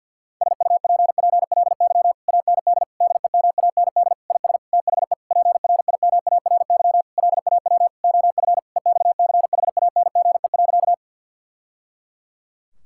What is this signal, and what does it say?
Morse code, heard as a keyed tone, about 52 wpm, 700 Hz. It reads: SUPPLY AND DEMAND IS THE PRIMARY LAW OF EXCHANGE.
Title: Sentences from Top 1000 Words - Review Part 6 - 50wpm